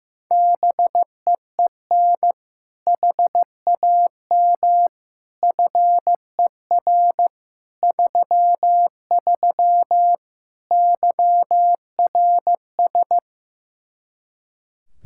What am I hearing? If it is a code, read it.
Morse code, heard as a keyed tone, about 16 wpm, 700 Hz. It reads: BEEN HAM FER 33 YRS
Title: Practice QSO - Exchange 3 - More Info - 15wpm